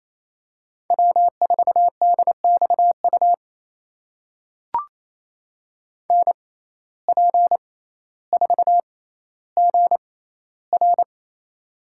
Morse code, heard as a keyed tone, about 28 wpm, spaced as for 14 wpm, 700 Hz. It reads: W4DXU NP4GR
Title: USA Call Signs - 5 Characters - 6x Character Spacing - 28wpm